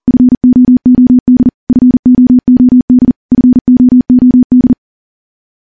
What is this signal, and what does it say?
Morse code, heard as a keyed tone, about 40 wpm, 250 Hz. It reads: FOOD FOOD FOOD
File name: morse / morse-training-40WPM